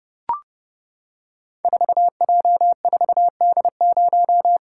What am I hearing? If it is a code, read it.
Morse code, heard as a keyed tone, about 30 wpm, 700 Hz. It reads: 4J4D0
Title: Code Groups - 5 Letter:Numbers - 30wpm